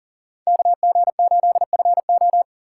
Morse code, heard as a keyed tone, about 40 wpm, 700 Hz. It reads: KG8FO
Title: USA Call Signs - 5 Characters - 4x Character Spacing - 40wpm